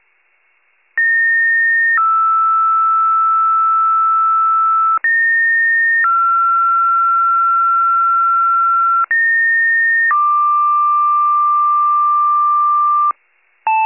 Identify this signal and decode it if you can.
Two-tone page, heard as a one-second tone, then a three-second tone, about 1850 then 1320 Hz, about 1850 then 1360 Hz, about 1850 then 1180 Hz. It is heard from 1.0 to 13.1 s.